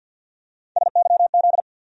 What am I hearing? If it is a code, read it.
Morse code, heard as a keyed tone, about 53 wpm, 700 Hz. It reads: SYZ
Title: USA Call Sign Postfixes - 3 Letters 50wpm